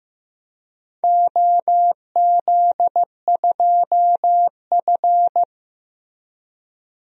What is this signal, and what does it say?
Morse code, heard as a keyed tone, about 15 wpm, 700 Hz. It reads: OZ2F
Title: DXCC-100 Call Signs - 4 Characters - Repeated 3x - 15wpm